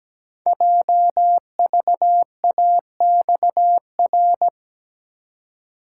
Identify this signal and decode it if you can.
Morse code, heard as a keyed tone, about 17 wpm, 700 Hz. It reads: JVAXR
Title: Code Groups - 5 Letter:Numbers - 17wpm